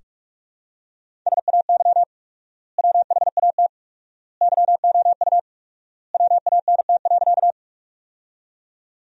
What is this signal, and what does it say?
Morse code, heard as a keyed tone, about 46 wpm, 700 Hz. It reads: SAY WHAT YOU WANT.
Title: 4x Word-Spacing - Sentences from Top 200 Words - 45wpm